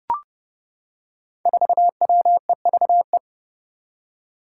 Morse code, heard as a keyed tone, about 28 wpm, 700 Hz. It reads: 4WEVE
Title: Five Challenging Letter-Number Combinations 30wpm